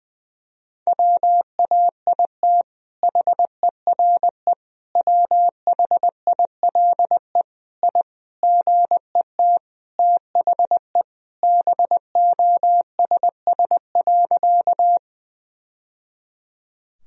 Morse code, heard as a keyed tone, about 20 wpm, 700 Hz. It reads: WAIT HERE WHILE I GET THE BOSS.
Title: Sentences from Top 1000 Words - Review Part 4 - 20wpm